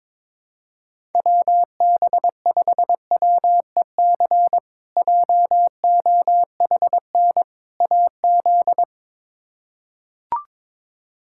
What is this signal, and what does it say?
Morse code, heard as a keyed tone, about 22 wpm, 700 Hz. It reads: WB5WEC JOHN AZ
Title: K1USN SST QSO Practice - 22wpm